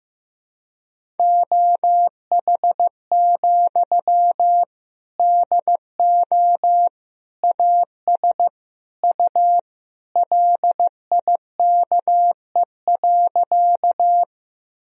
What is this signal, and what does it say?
Morse code, heard as a keyed tone, about 15 wpm, 700 Hz. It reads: OH, DO AS U LIKE.
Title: Sentences from Top 500 Words and Abbreviations 15wpm